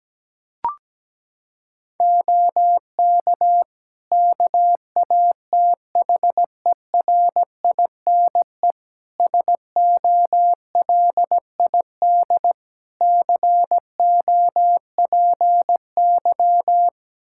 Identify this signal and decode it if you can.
Morse code, heard as a keyed tone, about 17 wpm, 700 Hz. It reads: OK KATHERINE SOLID COPY
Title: Practice QSO - Exchange 3 - More Info - 17wpm